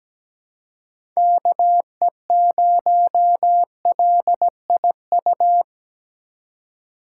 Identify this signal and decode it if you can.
Morse code, heard as a keyed tone, about 17 wpm, 700 Hz. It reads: KE0LIU
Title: USA Call Signs - 6 Characters - 2x Character Spacing - 17wpm